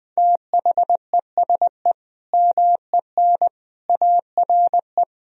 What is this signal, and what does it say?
Morse code, heard as a keyed tone, about 20 wpm, 700 Hz. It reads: THESE MEN ARE
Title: Sentences from Top 1000 Words - Review Part 1 - 20wpm